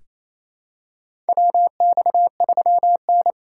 Morse code, heard as a keyed tone, about 27 wpm, 700 Hz. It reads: WX3N
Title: USA Call Signs - 2x Character Spacing - 28wpm